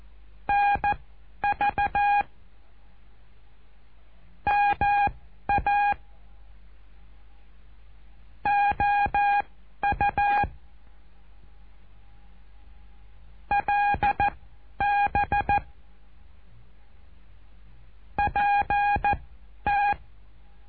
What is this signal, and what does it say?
Morse code, heard as a keyed tone, about 14 wpm, spaced as for 10 wpm, 820 Hz. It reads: NV MA OU LB PT
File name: cursos / CURSO DE TELEGRAFIA / Telegrafia - Trilha 08